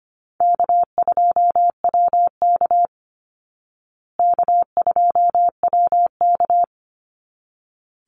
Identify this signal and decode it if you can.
Morse code, heard as a keyed tone, about 25 wpm, 700 Hz. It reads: K2WK K2WK
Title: Easy Call Signs Repeated 2x - 25wpm